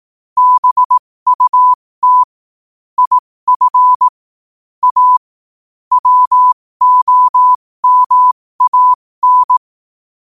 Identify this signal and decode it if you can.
Morse code, heard as a keyed tone, about 18 wpm, spaced as for 15 wpm, 1000 Hz. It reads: BUT IF A WOMAN
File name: qotdfiles / 15wpm